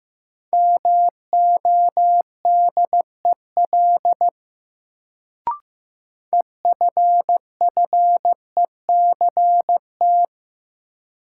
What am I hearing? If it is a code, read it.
Morse code, heard as a keyed tone, about 16 wpm, 700 Hz. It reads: MODEL EFFECT
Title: Sentences from Top 1000 Words - New Word Drill - 15wpm